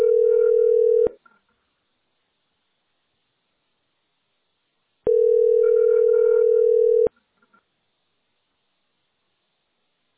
Ringback tone (North America), 440 + 480 Hz.